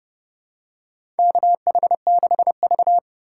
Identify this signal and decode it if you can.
Morse code, heard as a keyed tone, about 30 wpm, 700 Hz. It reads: KH6V
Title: USA Call Signs - 4 Characters - 4x Character Spacing - 30wpm